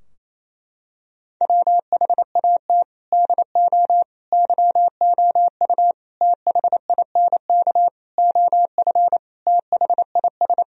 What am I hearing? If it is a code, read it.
Morse code, heard as a keyed tone, about 28 wpm, 700 Hz. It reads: WHAT DO YOU THINK OF THIS
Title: Sentences from Top 1000 Words - Review Part 2 - 28wpm